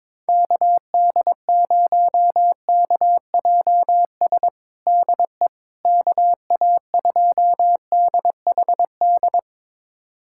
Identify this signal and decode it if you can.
Morse code, heard as a keyed tone, about 22 wpm, 700 Hz. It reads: KD0KJS DE KA2DHD
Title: USA Callsign DE Callsign - 6 Characters - 22wpm